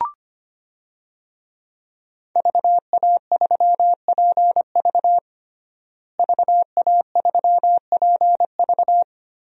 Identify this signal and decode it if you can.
Morse code, heard as a keyed tone, about 25 wpm, 700 Hz. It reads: VA3PV VA3PV
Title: DXCC-100 Call Signs - 5 Characters - Repeated 3x - 25wpm